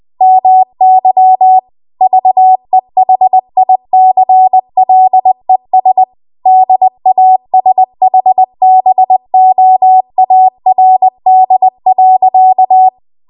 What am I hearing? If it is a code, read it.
Morse code, heard as a keyed tone, about 20 wpm, 750 Hz. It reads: MY VEHICLES DASHBOARD.
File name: Archive / 20 WPM / 20WPM